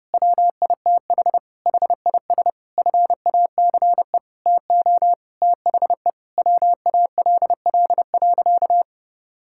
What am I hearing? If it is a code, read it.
Morse code, heard as a keyed tone, about 30 wpm, 700 Hz. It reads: WITH HIS FACE TO THE WALL.
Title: Sentences from Top 1000 Words - Review Part 6 - 30wpm